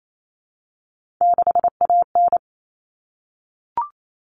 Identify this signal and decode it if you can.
Morse code, heard as a keyed tone, about 28 wpm, 700 Hz. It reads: 6AN
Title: Common Cut Numbers - 3-Digit 28wpm